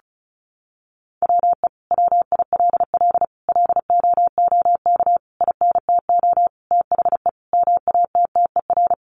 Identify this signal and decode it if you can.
Morse code, heard as a keyed tone, about 34 wpm, 700 Hz. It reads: WE WILL LOOK INTO THE MATTER
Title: Sentences from Top 1400 Words 35wpm